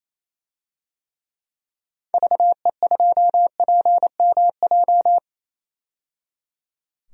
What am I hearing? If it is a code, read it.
Morse code, heard as a keyed tone, about 28 wpm, 700 Hz. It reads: VE2PMJ
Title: DXCC-100 Call Signs - 6 Characters - 28wpm